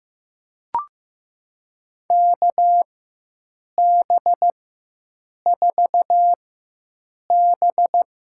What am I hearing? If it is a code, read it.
Morse code, heard as a keyed tone, about 15 wpm, spaced as for 7 wpm, 700 Hz. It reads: KB4B
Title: USA Call Signs - 6 Characters - 4x Character Spacing - 15wpm